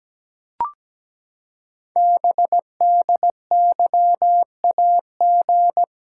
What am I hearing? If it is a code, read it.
Morse code, heard as a keyed tone, about 17 wpm, 700 Hz. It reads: BDYAG
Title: Code Groups - 5 Letters - 17wpm